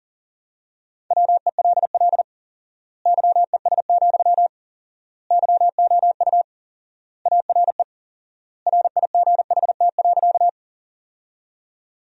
Morse code, heard as a keyed tone, about 37 wpm, 700 Hz. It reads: WELL YES, YOU ARE RIGHT.
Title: 4x Word-Spacing - Sentences from Top 500 Words - 40wpm